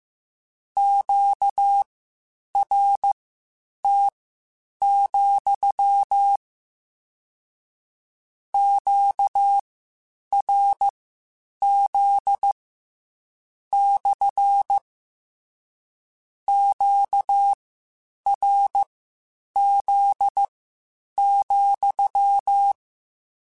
Morse code, heard as a keyed tone, about 15 wpm, spaced as for 8 wpm, 800 Hz. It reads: QRT, QRZ / QRZ,
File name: Q Calls / Q Calls-mp3s / Q CALLS SPSL @13